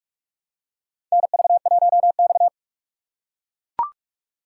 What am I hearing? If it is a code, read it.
Morse code, heard as a keyed tone, about 41 wpm, 700 Hz. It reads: NV1X
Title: USA Call Signs - 4 Characters - 3x Character Spacing - 45wpm